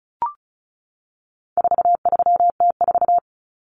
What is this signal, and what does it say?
Morse code, heard as a keyed tone, about 34 wpm, 700 Hz. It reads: 43T4
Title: Common Cut Numbers - 4-Digit 35wpm